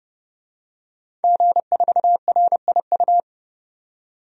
Morse code, heard as a keyed tone, about 32 wpm, 700 Hz. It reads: G4RIU
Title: DXCC-100 Call Signs - 5 Characters - 30wpm